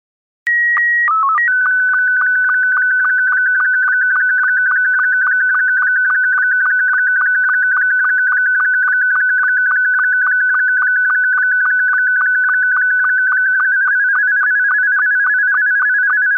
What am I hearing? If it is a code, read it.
SSTV (slow-scan television), Scottie 2.
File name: sdr / sstv